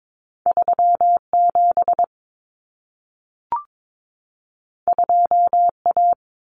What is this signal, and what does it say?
Morse code, heard as a keyed tone, about 22 wpm, 700 Hz. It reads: 37 2A
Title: Common Cut Numbers - 2-Digit 22wpm